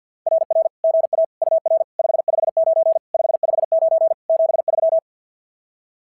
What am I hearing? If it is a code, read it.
Morse code, heard as a keyed tone, about 46 wpm, 660 Hz. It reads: RR GA UR 559 559 73
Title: SOTA QSO Practice 50wpm